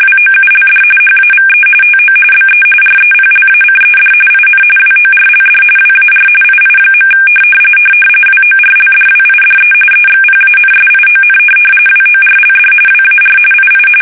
RTTY, about 45.45 baud. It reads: WOULD HAVE TO BE REPLACED IN WORLD WAR II. WORLDWIDE EXPANSION WORLD WAR II
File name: fsk850